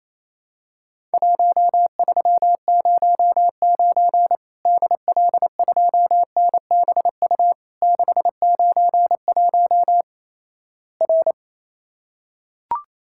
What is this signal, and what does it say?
Morse code, heard as a keyed tone, about 28 wpm, 700 Hz. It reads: 1309 DL2NBU 691 R
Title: Worked All Europe DX Contest - QTC Exchange - 28wpm